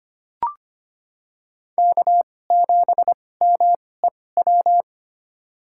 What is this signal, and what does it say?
Morse code, heard as a keyed tone, about 25 wpm, spaced as for 18 wpm, 700 Hz. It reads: K7MEW
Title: USA Call Signs - 5 Characters - 2x Character Spacing - 25wpm